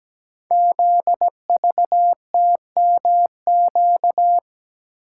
Morse code, heard as a keyed tone, about 17 wpm, 700 Hz. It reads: ZVTMQ
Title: Code Groups - 5 Letters - 17wpm